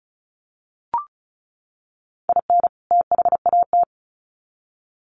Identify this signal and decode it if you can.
Morse code, heard as a keyed tone, about 34 wpm, 700 Hz. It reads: IN THAT
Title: Sets of 2 Words 35wpm